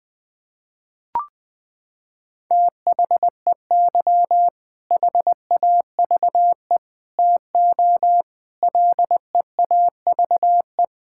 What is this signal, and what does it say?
Morse code, heard as a keyed tone, about 21 wpm, 700 Hz. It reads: THEY HAVE TO LEAVE
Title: Sentences with 3-Word Phrases - 20wpm